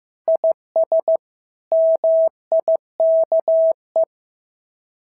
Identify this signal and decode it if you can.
Morse code, heard as a keyed tone, about 14 wpm, 650 Hz. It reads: IS MIKE
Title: SOTA QSO Practice 15wpm